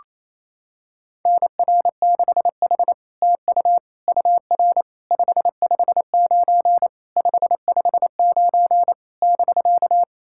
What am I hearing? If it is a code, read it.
Morse code, heard as a keyed tone, about 27 wpm, 700 Hz. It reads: NR6H TU UR 559 559 <BK>
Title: SOTA QSO Practice 28wpm